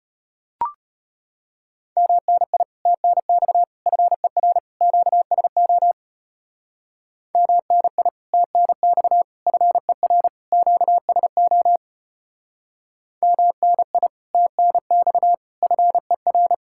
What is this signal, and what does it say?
Morse code, heard as a keyed tone, about 32 wpm, 700 Hz. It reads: MNI TNX FER QSO MNI TNX FER QSO MNI TNX FER